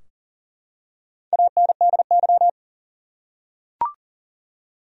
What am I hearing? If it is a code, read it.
Morse code, heard as a keyed tone, about 40 wpm, 700 Hz. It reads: ANDY
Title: USA Common Names - Max 4 Letters - 40wpm